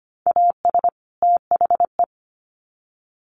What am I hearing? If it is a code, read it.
Morse code, heard as a keyed tone, about 25 wpm, 700 Hz. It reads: AS THE
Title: Sets of 2 Words 25wpm